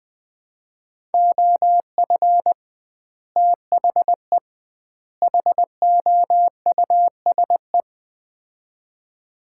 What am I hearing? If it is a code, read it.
Morse code, heard as a keyed tone, about 20 wpm, 700 Hz. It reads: OF THE HOUSE
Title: 2x Word-Spacing - Sets of 3 Words - 20wpm